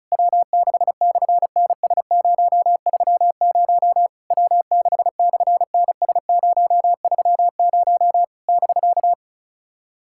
Morse code, heard as a keyed tone, about 35 wpm, 700 Hz. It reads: W6/NS030 W6/NS030 <BK>
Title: SOTA QSO Practice 35wpm